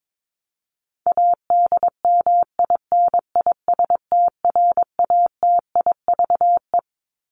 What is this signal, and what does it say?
Morse code, heard as a keyed tone, about 22 wpm, 700 Hz. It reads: ADMINISTRATIVE
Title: Top 1501-1750 Words 22wpm